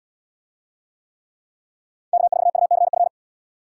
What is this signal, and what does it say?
Morse code, heard as a keyed tone, about 64 wpm, 700 Hz. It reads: K4UCF